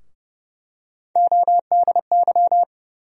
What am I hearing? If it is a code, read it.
Morse code, heard as a keyed tone, about 30 wpm, 700 Hz. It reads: ODY
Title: USA Call Sign Postfixes - 3 Letters 30wpm